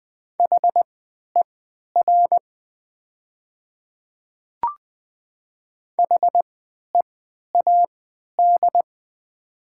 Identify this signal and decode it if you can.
Morse code, heard as a keyed tone, about 20 wpm, spaced as for 11 wpm, 700 Hz. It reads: HER HEAD
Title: 3x Character Spacing - Top 300 Words - 20wpm